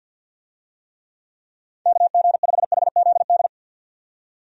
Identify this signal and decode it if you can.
Morse code, heard as a keyed tone, about 52 wpm, 700 Hz. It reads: KG5HZD
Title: USA Call Signs - 6 Characters - 50wpm